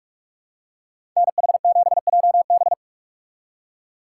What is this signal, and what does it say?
Morse code, heard as a keyed tone, about 44 wpm, 700 Hz. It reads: NH7JB